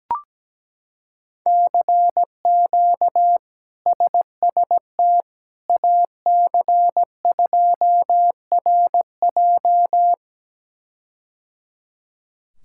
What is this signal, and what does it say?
Morse code, heard as a keyed tone, about 17 wpm, 700 Hz. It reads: CQ SST AC2RJ
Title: K1USN SST QSO Practice - 17wpm